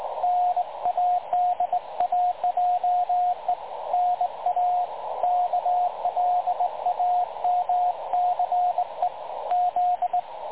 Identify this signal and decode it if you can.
Morse code, heard as a keyed tone, about 25 wpm, 720 Hz, with radio noise. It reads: NADAJE NA KLAMCE Z
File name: dlazaawansowanych / nagraniazpasma / websdr recording 3565.0kHz